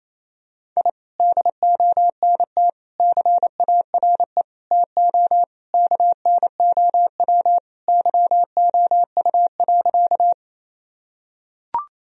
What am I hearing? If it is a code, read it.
Morse code, heard as a keyed tone, about 28 wpm, 700 Hz. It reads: I DONT CARE TO KNOW YOU.
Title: Sentences from Top 1000 Words - Review Part 1 - 28wpm